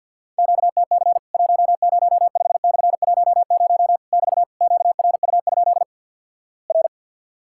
Morse code, heard as a keyed tone, about 50 wpm, 700 Hz. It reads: QTC 105/10 <BT> QRU? R
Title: Worked All Europe DX Contest - QTC Exchange - 50wpm